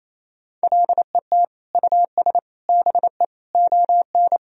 Morse code, heard as a keyed tone, about 28 wpm, 700 Hz. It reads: LET US BE ON
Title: Sentences from Top 1000 Words - Review Part 4 - 28wpm